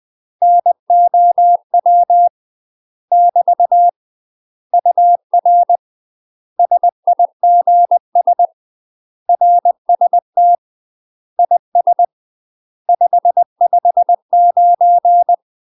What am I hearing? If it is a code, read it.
Morse code, heard as a keyed tone, about 20 wpm, 700 Hz. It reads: NOW <BT> UR SIGS RST IS 559
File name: UK MASTER QSOs TRAINING/20WPM